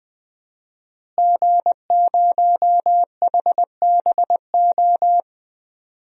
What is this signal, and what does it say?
Morse code, heard as a keyed tone, about 21 wpm, 700 Hz. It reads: G0HBO
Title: DXCC-100 Call Signs - 5 Characters - Repeated 3x - 20wpm